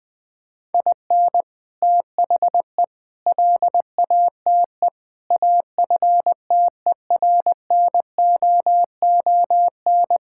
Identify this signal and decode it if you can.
Morse code, heard as a keyed tone, about 20 wpm, 700 Hz. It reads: IN THE LATE AFTERNOON
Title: Sentences with 3-Word Phrases - 20wpm